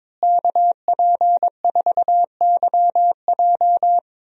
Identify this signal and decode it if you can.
Morse code, heard as a keyed tone, about 22 wpm, 700 Hz. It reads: KP4YJ
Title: DXCC-100 Call Signs - 5 Characters - Repeated 2x - 22wpm